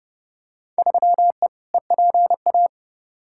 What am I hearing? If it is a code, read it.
Morse code, heard as a keyed tone, about 30 wpm, 700 Hz. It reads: 3E EPA
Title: Field Day Exchanges Encore - 30wpm